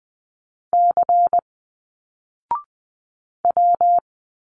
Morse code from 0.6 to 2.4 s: C